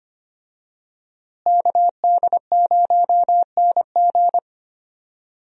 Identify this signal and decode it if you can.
Morse code, heard as a keyed tone, about 25 wpm, 700 Hz. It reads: KD0NG
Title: USA Call Signs - 5 Characters - Repeated 3x - 25wpm